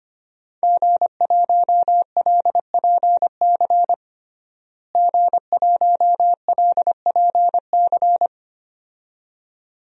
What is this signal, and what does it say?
Morse code, heard as a keyed tone, about 25 wpm, 700 Hz. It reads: G1LPC G1LPC
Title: DXCC-100 Call Signs - 5 Characters - Repeated 2x - 25wpm